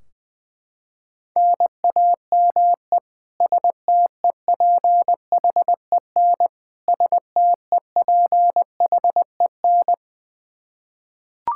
Morse code, heard as a keyed tone, about 20 wpm, 700 Hz. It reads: NAME STEPHEN STEPHEN
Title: Practice QSO - Exchange 1 - Beginning - 20wpm